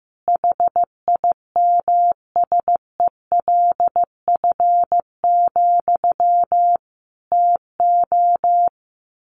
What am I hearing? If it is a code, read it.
Morse code, heard as a keyed tone, about 15 wpm, 700 Hz. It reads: HIMSELF, TO
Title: A Sound Of Thunder 15wpm